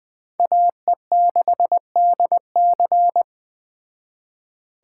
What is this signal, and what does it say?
Morse code, heard as a keyed tone, about 21 wpm, 700 Hz. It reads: AE6DC
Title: Speed Racing - USA 2x2 Call Signs 17wpm